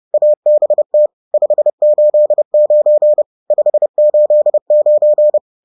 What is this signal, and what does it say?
Morse code, heard as a keyed tone, about 30 wpm, 600 Hz. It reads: ABT 589 589
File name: 30wpm-7